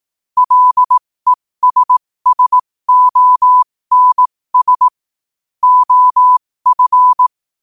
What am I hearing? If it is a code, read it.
Morse code, heard as a keyed tone, about 18 wpm, spaced as for 15 wpm, 1000 Hz. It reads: LESSONS OF
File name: qotdfiles / 15wpm